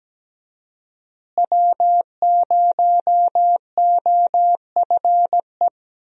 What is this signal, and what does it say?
Morse code, heard as a keyed tone, about 17 wpm, 700 Hz. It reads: W0OFE
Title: USA Call Signs - 5 Characters - Repeated 3x - 17wpm